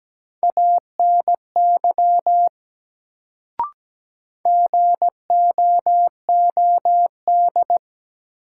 Morse code, heard as a keyed tone, about 17 wpm, 700 Hz. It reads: ANY GOOD
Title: Sentences from Top 200 Words - New Word Drill - 17wpm